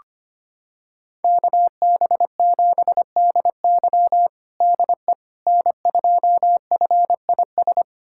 Morse code, heard as a keyed tone, about 25 wpm, 700 Hz. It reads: KB7DY DE N2FIS
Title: USA Callsign DE Callsign - 5 Characters - 25wpm